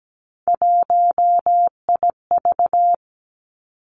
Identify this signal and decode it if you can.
Morse code, heard as a keyed tone, about 17 wpm, 700 Hz. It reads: 1IV
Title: Three Challenging Letter-Number Combinations 17wpm